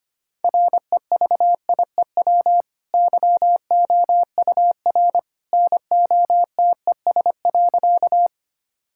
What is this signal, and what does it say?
Morse code, heard as a keyed tone, about 25 wpm, 700 Hz. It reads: REVIEW YOUR NOTES.
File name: Easy Sentences with Hints - 25wpm